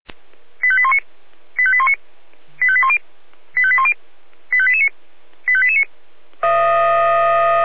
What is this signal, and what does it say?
Five-tone selective call (ZVEI), heard as quick runs of five tones, tones of about 70 ms. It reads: 85819 85819 85810 85810 85889 85889